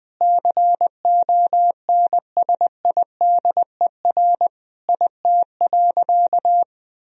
Morse code, heard as a keyed tone, about 20 wpm, 700 Hz. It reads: CONSIDER IT.
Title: Sentences from Top 1000 Words - Review Part 4 - 20wpm